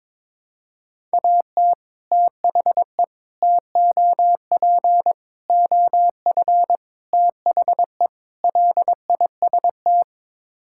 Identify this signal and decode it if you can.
Morse code, heard as a keyed tone, about 22 wpm, 700 Hz. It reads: AT THE TOP OF THE LIST
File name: practice / 22wpm / Sets of 6 Words - 22wpm